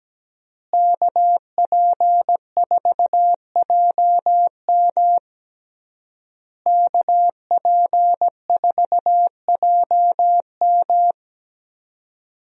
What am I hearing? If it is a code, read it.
Morse code, heard as a keyed tone, about 17 wpm, 700 Hz. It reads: KP4JM KP4JM
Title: DXCC-100 Call Signs - 5 Characters - Repeated 3x - 17wpm